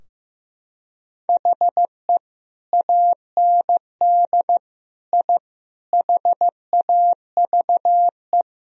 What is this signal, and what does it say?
Morse code, heard as a keyed tone, about 15 wpm, 700 Hz. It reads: HE AND I HAVE
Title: Sentences from Top 1000 Words - Review Part 3 - 15wpm